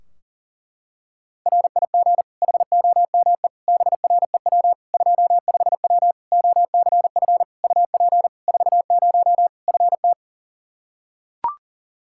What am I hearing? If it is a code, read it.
Morse code, heard as a keyed tone, about 40 wpm, 700 Hz. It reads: RIG HOME BREW 25W OCF UP 40 FT
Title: Practice QSO - Exchange 2 - Info - 40wpm